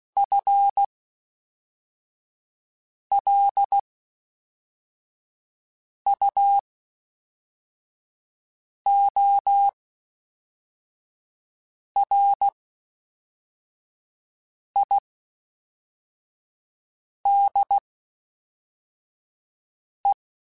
Morse code, heard as a keyed tone, about 16 wpm, spaced as for 4 wpm, 800 Hz. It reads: FLUORIDE
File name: Spaced / FL Runs sp mp3s / FL Runs sp @5